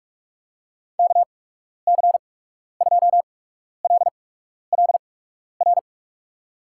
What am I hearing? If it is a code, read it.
Morse code, heard as a keyed tone, about 41 wpm, spaced as for 12 wpm, 700 Hz. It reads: KC2LLR
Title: USA Call Signs - 6 Characters - 8x Character Spacing - 45wpm